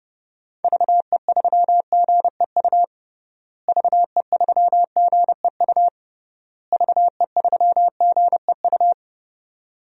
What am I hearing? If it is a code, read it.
Morse code, heard as a keyed tone, about 32 wpm, 700 Hz. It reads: VE3GEU VE3GEU VE3GEU
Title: DXCC-100 Call Signs - 6 Characters - Repeated 3x - 30wpm